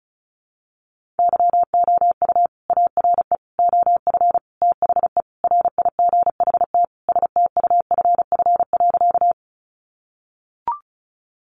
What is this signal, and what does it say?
Morse code, heard as a keyed tone, about 34 wpm, 700 Hz. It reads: YOU ARE OF THE RIGHT STUFF.
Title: Sentences from Top 1400 Words 35wpm